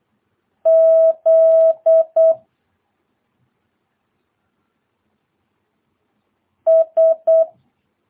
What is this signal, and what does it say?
Morse code, heard as a keyed tone, about 8 wpm, 650 Hz. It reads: Z S
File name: jh8cbh / 2022cw